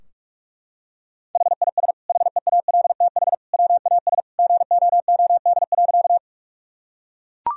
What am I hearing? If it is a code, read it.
Morse code, heard as a keyed tone, about 44 wpm, 700 Hz. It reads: HIS HEALTH WAS GOOD.
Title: Sentences from Top 1000 Words - Review Part 3 - 45wpm